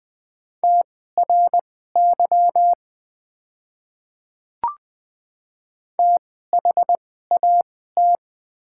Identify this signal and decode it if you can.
Morse code, heard as a keyed tone, about 20 wpm, spaced as for 15 wpm, 700 Hz. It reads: TRY THAT
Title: 2x Character Spacing - Top 200 Words - 20wpm